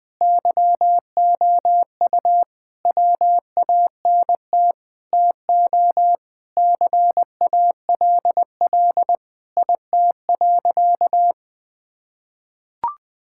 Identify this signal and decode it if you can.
Morse code, heard as a keyed tone, about 20 wpm, 700 Hz. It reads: YOU WANT TO CALL IT.
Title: Sentences with 7-Word Phrases - 20wpm